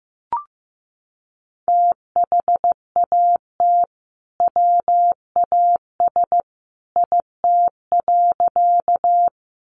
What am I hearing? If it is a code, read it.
Morse code, heard as a keyed tone, about 15 wpm, 700 Hz. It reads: THAT WAS IT.
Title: Sentences from Top 100 Words 15wpm